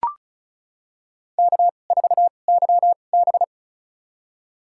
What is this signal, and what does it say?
Morse code, heard as a keyed tone, about 36 wpm, spaced as for 25 wpm, 700 Hz. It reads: K4YB
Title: USA Call Signs - 4 Characters - 2x Character Spacing - 35wpm